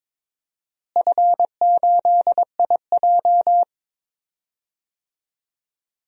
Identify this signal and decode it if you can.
Morse code, heard as a keyed tone, about 22 wpm, 700 Hz. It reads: F8IJ
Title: DXCC-100 Call Signs - 4 Characters - Repeated 3x - 22wpm